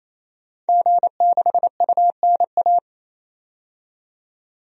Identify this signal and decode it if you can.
Morse code, heard as a keyed tone, about 28 wpm, 700 Hz. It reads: G6UNA
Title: DXCC-100 Call Signs - 5 Characters - 28wpm